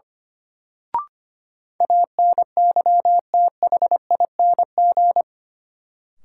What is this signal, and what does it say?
Morse code, heard as a keyed tone, about 25 wpm, 700 Hz. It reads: ANYTHING
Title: Sets of 4 Words - New Word Drill - 25wpm